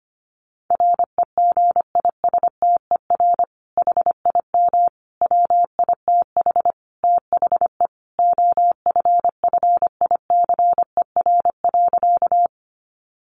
Morse code, heard as a keyed tone, about 25 wpm, 700 Hz. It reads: REGISTER HIM WITH THE OFFICER.
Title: Sentences from Top 1200 Words 25wpm